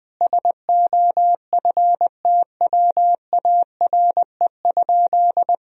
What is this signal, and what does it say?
Morse code, heard as a keyed tone, about 21 wpm, 700 Hz. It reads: SOFTWARE?
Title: Sentences from Top 1000 Words - Review Part 3 - 20wpm